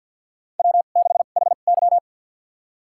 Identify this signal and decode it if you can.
Morse code, heard as a keyed tone, about 53 wpm, spaced as for 37 wpm, 700 Hz. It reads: W6HY
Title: USA Call Signs - 4 Characters - 2x Character Spacing - 50wpm